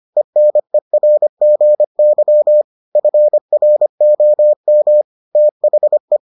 Morse code, heard as a keyed tone, about 25 wpm, 600 Hz. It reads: ENERGY FROM THE